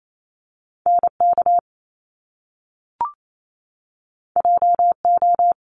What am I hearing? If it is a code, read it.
Morse code, heard as a keyed tone, about 28 wpm, 700 Hz. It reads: NK JO
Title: Top Two Letter Combinations in Words 28wpm